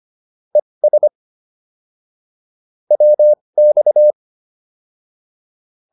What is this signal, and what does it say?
Morse code, heard as a keyed tone, about 25 wpm, spaced as for 20 wpm, 600 Hz. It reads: ES WX